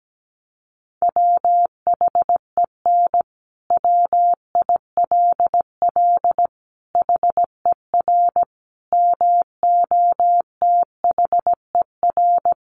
Morse code, heard as a keyed tone, about 17 wpm, 700 Hz. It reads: WHEN WILL HER MOTHER
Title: Sentences from Top 200 Words 17wpm